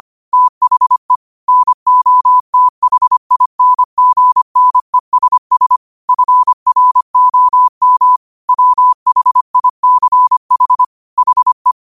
Morse code, heard as a keyed tone, about 25 wpm, 1000 Hz. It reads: THE NOTHINGNESS FROM WHICH HE
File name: qotdfiles / 25wpm